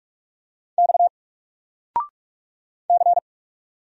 Morse code, heard as a keyed tone, about 44 wpm, 700 Hz. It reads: X C
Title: Single Letter-Number 45wpm